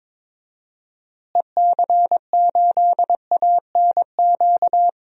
Morse code, heard as a keyed tone, about 22 wpm, 700 Hz. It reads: EC8ANQ
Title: DXCC-100 Call Signs - 6 Characters - Repeated 2x - 22wpm